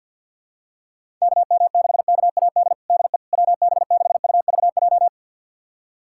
Morse code, heard as a keyed tone, about 46 wpm, 700 Hz. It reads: KM6KAD DE WB6UVJ
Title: USA Callsign DE Callsign - 6 Characters - 50wpm